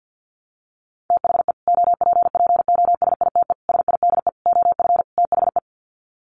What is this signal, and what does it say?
Morse code, heard as a keyed tone, about 50 wpm, 700 Hz. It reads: THE OPPOSITE SIDE OF THE
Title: Sets of 5 Words Encore 50wpm